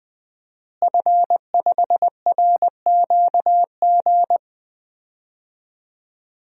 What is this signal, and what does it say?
Morse code, heard as a keyed tone, about 20 wpm, 700 Hz. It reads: F5RQG
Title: DXCC-100 Call Signs - 5 Characters - Repeated 3x - 20wpm